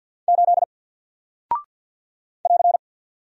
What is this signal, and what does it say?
Morse code, heard as a keyed tone, about 48 wpm, 700 Hz. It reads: 8 <AR>
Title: Single Character (Letter:Number:Punctuation:Prosign) Alt 50wpm